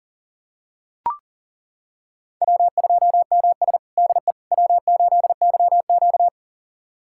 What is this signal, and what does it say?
Morse code, heard as a keyed tone, about 37 wpm, 700 Hz. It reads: W2MS DE W8YQ
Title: USA Callsign DE Callsign - 4 Characters - 40wpm